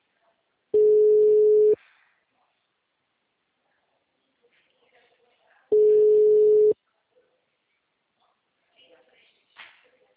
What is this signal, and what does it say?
Ringback tone (Europe, 425 Hz), with speech.